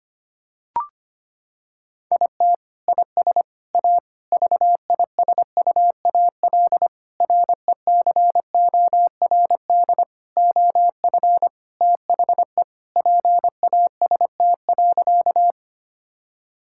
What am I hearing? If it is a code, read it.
Morse code, heard as a keyed tone, about 25 wpm, 700 Hz. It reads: IT IS A VISUAL RECORD OF THE PAST.
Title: Sentences from Top 1000 Words - Review Part 6 - 25wpm